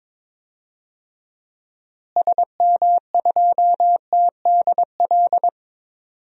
Morse code, heard as a keyed tone, about 22 wpm, 700 Hz. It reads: SM2TDL
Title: DXCC-100 Call Signs - 6 Characters - Repeated 3x - 22wpm